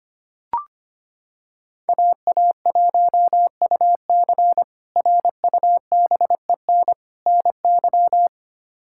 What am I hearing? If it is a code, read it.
Morse code, heard as a keyed tone, about 25 wpm, 700 Hz. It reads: AA1UC RUBEN NY
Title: K1USN SST QSO Practice - 25wpm